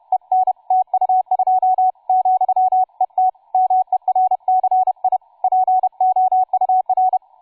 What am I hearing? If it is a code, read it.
Morse code, heard as a keyed tone, about 31 wpm, 750 Hz, with radio noise. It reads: EN TU2 , ET MERCI POUR
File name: cw forum / tu2 f5ldy 18mhz 16h46